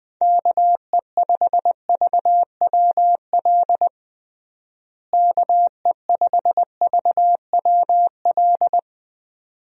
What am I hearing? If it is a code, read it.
Morse code, heard as a keyed tone, about 20 wpm, 700 Hz. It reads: KE5VWL KE5VWL
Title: USA Call Signs - 6 Characters - Repeated 2x - 20wpm